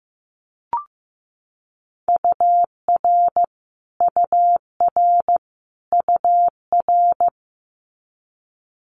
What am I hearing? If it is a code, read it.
Morse code, heard as a keyed tone, about 15 wpm, 700 Hz. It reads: UR UR UR
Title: Abbreviations Repeated 3x - 15wpm